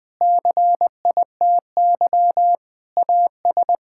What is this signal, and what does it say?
Morse code, heard as a keyed tone, about 20 wpm, 700 Hz. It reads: CITY AS
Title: Pangram Challenge v2 - 20wpm